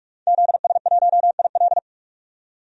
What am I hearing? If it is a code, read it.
Morse code, heard as a keyed tone, about 46 wpm, 700 Hz. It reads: ZS1IL
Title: DXCC-100 Call Signs - 5 Characters - 45wpm